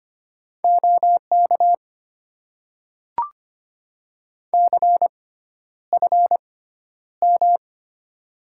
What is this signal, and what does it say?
Morse code, heard as a keyed tone, about 25 wpm, spaced as for 11 wpm, 700 Hz. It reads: OK CFM
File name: Top 90 QSO Elements - 6x Character Spacing - 25wpm